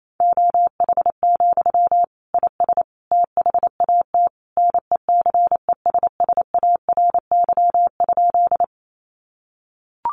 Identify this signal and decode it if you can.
Morse code, heard as a keyed tone, about 28 wpm, 700 Hz. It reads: OH, IS THAT NECESSARY?
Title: Sentences from Top 800 Words 28wpm